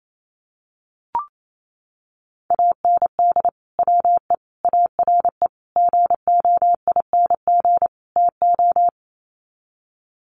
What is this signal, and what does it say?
Morse code, heard as a keyed tone, about 28 wpm, 700 Hz. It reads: AND WE ARE GOING TO
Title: Sets of 5 Words Encore2 28wpm